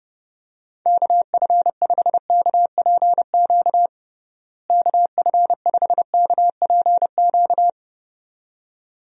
Morse code, heard as a keyed tone, about 30 wpm, 700 Hz. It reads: KF5KPQ KF5KPQ
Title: USA Call Signs - 6 Characters - Repeated 2x - 30wpm